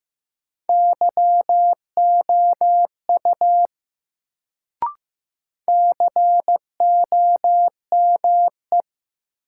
Morse code, heard as a keyed tone, about 15 wpm, 700 Hz. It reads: YOU COME
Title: Sets of 2 Words - New Word Drill - 15wpm